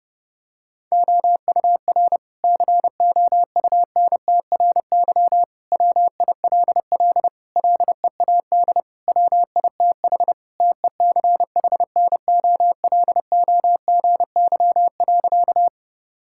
Morse code, heard as a keyed tone, about 30 wpm, 700 Hz. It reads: OUR COUNTRY WILL LEAD WITH TECHNOLOGY.
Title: Sentences from Top 1000 Words - Review Part 5 - 30wpm